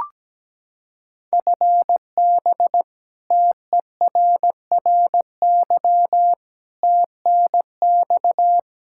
Morse code, heard as a keyed tone, about 17 wpm, 700 Hz. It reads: FB TERRY TNX
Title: Practice QSO - Exchange 3 - More Info - 17wpm